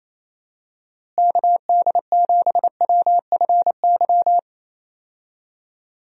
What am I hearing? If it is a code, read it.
Morse code, heard as a keyed tone, about 28 wpm, 700 Hz. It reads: KD7WFY
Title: USA Call Signs - 6 Characters - Repeated 2x - 28wpm